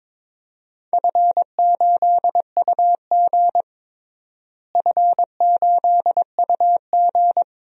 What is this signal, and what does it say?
Morse code, heard as a keyed tone, about 22 wpm, 700 Hz. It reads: F8UG F8UG
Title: DXCC-100 Call Signs - 4 Characters - Repeated 2x - 22wpm